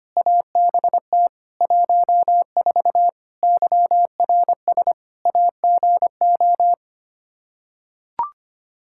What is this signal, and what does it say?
Morse code, heard as a keyed tone, about 25 wpm, 700 Hz. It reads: ABT 14 YRS AGO
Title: Practice QSO - Exchange 3 - More Info - 25wpm